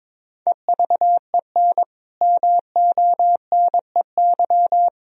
Morse code, heard as a keyed tone, about 22 wpm, 700 Hz. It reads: EVEN MONEY
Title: Sentences from Top 1000 Words - Review Part 4 - 22wpm